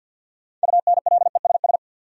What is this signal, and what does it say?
Morse code, heard as a keyed tone, about 52 wpm, 700 Hz. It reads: UNLESS
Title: Sentences from Top 1000 Words - Review Part 5 - New Word Drill - 50wpm